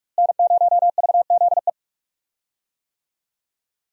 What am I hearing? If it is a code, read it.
Morse code, heard as a keyed tone, about 46 wpm, 700 Hz. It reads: N0VZE
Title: USA Call Signs - 5 Characters - Repeated 3x - 45wpm